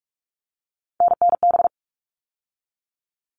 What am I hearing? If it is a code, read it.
Morse code, heard as a keyed tone, about 44 wpm, 700 Hz. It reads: NNB
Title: All Cut Numbers - 3-Digit 45wpm